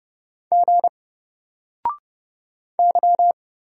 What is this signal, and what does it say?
Morse code, heard as a keyed tone, about 30 wpm, 700 Hz. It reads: G Y
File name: Single-Letter-Number-30wpm